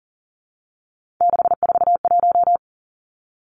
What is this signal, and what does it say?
Morse code, heard as a keyed tone, about 39 wpm, 700 Hz. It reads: 641
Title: Three Challenging Letter-Number Combinations 40wpm